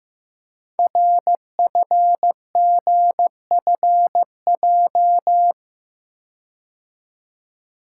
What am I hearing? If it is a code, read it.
Morse code, heard as a keyed tone, about 15 wpm, 700 Hz. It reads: RFGFJ
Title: Code Groups - 5 Letters - 15wpm